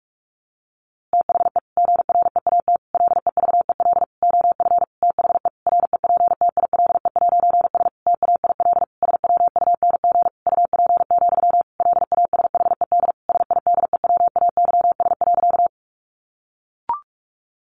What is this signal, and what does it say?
Morse code, heard as a keyed tone, about 44 wpm, 700 Hz. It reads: THE GREAT LEVEL OF THE REPTILE'S TAIL SWUNG UP, LASHED SIDEWAYS.
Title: A Sound Of Thunder 45wpm